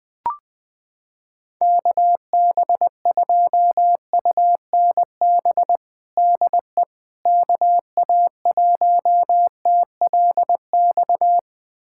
Morse code, heard as a keyed tone, about 21 wpm, 700 Hz. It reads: KB2UNB DE KA1TLX
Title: USA Callsign DE Callsign - 6 Characters - 20wpm